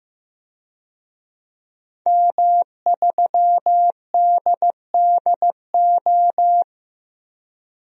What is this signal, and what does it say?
Morse code, heard as a keyed tone, about 15 wpm, 700 Hz. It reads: M3DDO
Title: DXCC-100 Call Signs - 5 Characters - Repeated 3x - 15wpm